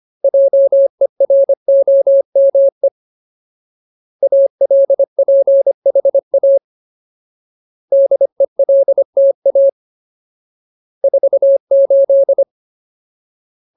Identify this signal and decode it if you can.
Morse code, heard as a keyed tone, about 25 wpm, 550 Hz. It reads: JEROME ALPHA DELTA 48
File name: Ultimate Word List / 25WPM Ultimate